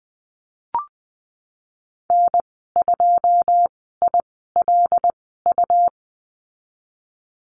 Morse code, heard as a keyed tone, about 20 wpm, spaced as for 15 wpm, 700 Hz. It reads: N2ILU
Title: 154 Lesson 25g - Letter p - Call Signs 20wpm